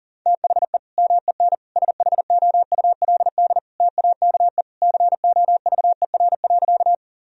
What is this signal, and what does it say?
Morse code, heard as a keyed tone, about 40 wpm, 700 Hz. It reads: THE MEN SHOULD TAKE COVER.
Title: Sentences from Top 1000 Words - Review Part 1 - 40wpm